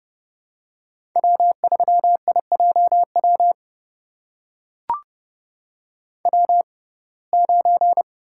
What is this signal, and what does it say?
Morse code, heard as a keyed tone, about 30 wpm, spaced as for 22 wpm, 700 Hz. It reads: W3IJW W9
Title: USA Call Signs - 5 Characters - 6x Character Spacing - 30wpm